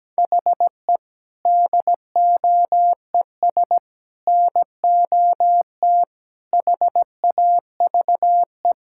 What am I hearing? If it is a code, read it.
Morse code, heard as a keyed tone, about 17 wpm, 700 Hz. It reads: HE DOES NOT HAVE
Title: Sentences from Top 1000 Words - Review Part 5 - 17wpm